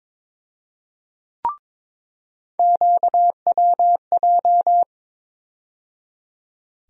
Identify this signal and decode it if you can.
Morse code, heard as a keyed tone, about 22 wpm, 700 Hz. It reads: QWJ
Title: USA Call Sign Postfixes - 3 Letters 22wpm